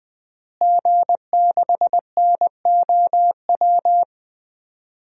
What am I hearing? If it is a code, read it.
Morse code, heard as a keyed tone, about 20 wpm, 700 Hz. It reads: G6NOW
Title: DXCC-100 Call Signs - 5 Characters - Repeated 3x - 20wpm